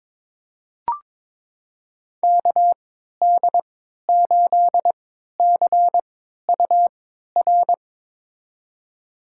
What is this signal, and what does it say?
Morse code, heard as a keyed tone, about 22 wpm, spaced as for 13 wpm, 700 Hz. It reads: KD8CUR
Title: USA Call Signs - 3x Character Spacing - 22wpm